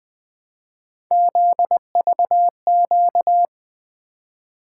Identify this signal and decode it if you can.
Morse code, heard as a keyed tone, about 20 wpm, 700 Hz. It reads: ZVQ
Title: Three Letter-Number Combinations - 20wpm